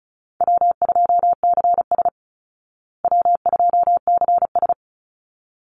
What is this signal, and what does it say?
Morse code, heard as a keyed tone, about 34 wpm, 700 Hz. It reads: W2CS W2CS
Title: Call Signs Repeated 3x - 35wpm